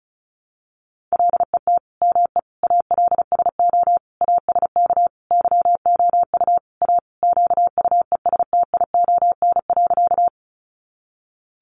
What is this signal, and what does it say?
Morse code, heard as a keyed tone, about 34 wpm, 700 Hz. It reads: LET ME ALSO ASK YOU A QUESTION.
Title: Sentences from Top 400 Words 35wpm